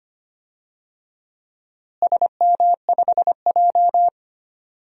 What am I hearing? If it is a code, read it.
Morse code, heard as a keyed tone, about 25 wpm, 700 Hz. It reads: SM5J
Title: DXCC-100 Call Signs - 4 Characters - Repeated 3x - 25wpm